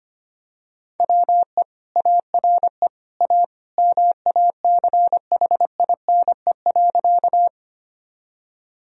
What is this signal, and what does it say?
Morse code, heard as a keyed tone, about 25 wpm, 700 Hz. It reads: WE ARE A MACHINE.
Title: Sentences from Top 1000 Words - Review Part 2 - 25wpm